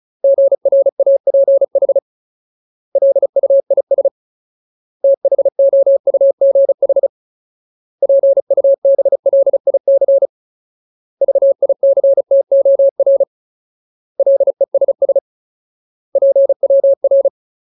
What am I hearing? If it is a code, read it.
Morse code, heard as a keyed tone, about 34 wpm, 550 Hz. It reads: GRAPH LUIS THOUGH PUBLIC VICTOR LESS PWR
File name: Ultimate Word List / 35WPM Ultimate